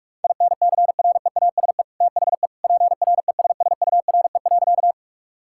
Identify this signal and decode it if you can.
Morse code, heard as a keyed tone, about 41 wpm, 700 Hz. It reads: INCREASE THE PRESSURE.
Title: Sentences from Top 1000 Words - Review Part 6 - 45wpm